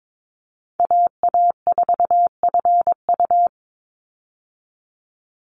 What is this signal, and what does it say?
Morse code, heard as a keyed tone, about 22 wpm, 700 Hz. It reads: AA4FU
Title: Call Signs Repeated 2x - 22wpm